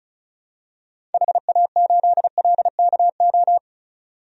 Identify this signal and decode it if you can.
Morse code, heard as a keyed tone, about 34 wpm, 700 Hz. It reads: HA8LKO
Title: DXCC-100 Call Signs - 6 Characters - 35wpm